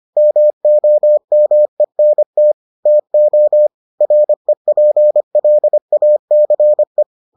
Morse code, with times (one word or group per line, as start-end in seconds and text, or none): 0.2-2.5 MOMENT
2.9-3.7 TO
4.0-7.0 REPLACE